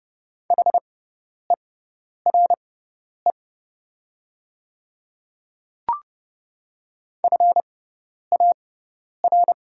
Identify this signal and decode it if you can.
Morse code, heard as a keyed tone, about 30 wpm, spaced as for 10 wpm, 690 Hz. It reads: HERE FAR
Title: 6x Character Spacing - Top 300 Words - 30wpm